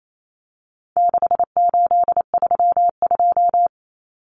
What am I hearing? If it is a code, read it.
Morse code, heard as a keyed tone, about 28 wpm, 700 Hz. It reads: 6832
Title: Numbers - 4-Digit 28wpm